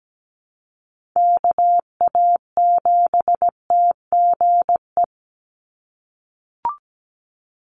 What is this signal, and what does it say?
Morse code, heard as a keyed tone, about 17 wpm, 700 Hz. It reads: KA7TGE
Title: USA Call Signs 17wpm